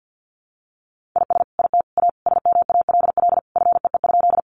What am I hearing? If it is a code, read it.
Morse code, heard as a keyed tone, about 50 wpm, 700 Hz. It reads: IS IT A SMALL FEE?
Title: Sentences from Top 1500 Words 50wpm